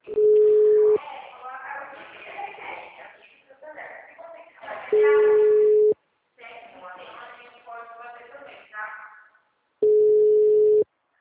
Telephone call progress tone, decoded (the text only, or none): ringback tone